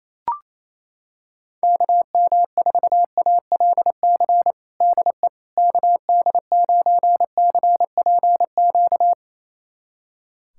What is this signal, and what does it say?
Morse code, heard as a keyed tone, about 28 wpm, 700 Hz. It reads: KM4ALC DE KD9CPQ
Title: USA Callsign DE Callsign 28wpm